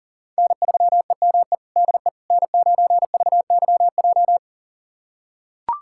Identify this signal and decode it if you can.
Morse code, heard as a keyed tone, about 40 wpm, 700 Hz. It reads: N3EME DE N9VYJ
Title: USA Callsign DE Callsign - 5 Characters - 40wpm